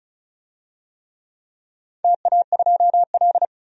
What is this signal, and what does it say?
Morse code, heard as a keyed tone, about 36 wpm, 700 Hz. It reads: TA2L
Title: DXCC-100 Call Signs - 4 Characters - 35wpm